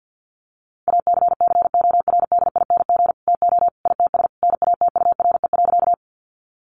Morse code, heard as a keyed tone, about 49 wpm, 700 Hz. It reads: ACCORDING TO ITS NATURE.
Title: Sentences from Top 700 Words 50wpm